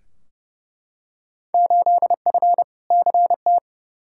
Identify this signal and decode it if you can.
Morse code, heard as a keyed tone, about 29 wpm, 700 Hz. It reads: 8F CT
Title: Field Day Exchanges Encore - 30wpm